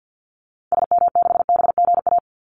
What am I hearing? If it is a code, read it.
Morse code, heard as a keyed tone, about 50 wpm, 700 Hz. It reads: SM6BGA
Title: Call Signs Repeated 2x - 50wpm